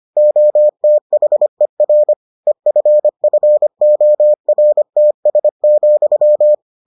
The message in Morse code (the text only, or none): OTHER EFFORTS,